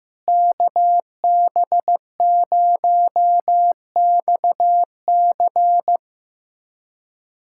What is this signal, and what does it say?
Morse code, heard as a keyed tone, about 15 wpm, 700 Hz. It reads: KB0XC
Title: USA Callsign DE Callsign - 5 Characters - 15wpm